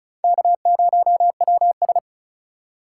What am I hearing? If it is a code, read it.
Morse code, heard as a keyed tone, about 46 wpm, 700 Hz. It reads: K0WS